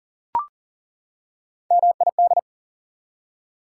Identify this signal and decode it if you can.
Morse code, heard as a keyed tone, about 40 wpm, 700 Hz. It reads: MID
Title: Prefixes and Suffixes 40wpm